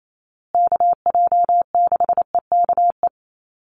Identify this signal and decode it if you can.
Morse code, heard as a keyed tone, about 28 wpm, 700 Hz. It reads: KJ6EKE
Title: USA Call Signs Repeated 2x 28wpm